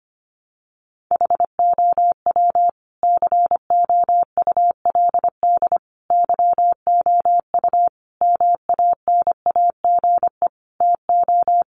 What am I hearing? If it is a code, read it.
Morse code, heard as a keyed tone, about 25 wpm, 700 Hz. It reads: HOW COULD YOU MANAGE TO
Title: Sentences from Top 1400 Words 25wpm